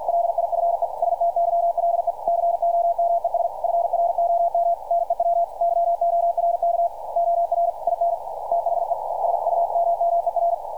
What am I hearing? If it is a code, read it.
Morse code, heard as a keyed tone, about 52 wpm, 690 Hz, with radio noise. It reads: LIFESTYLE COMES FROM NEW YORK OR IN THE SUMMER
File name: archived / CRSnet